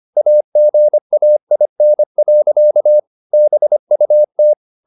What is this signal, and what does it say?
Morse code, heard as a keyed tone, about 25 wpm, 600 Hz. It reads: AGAIN. BUT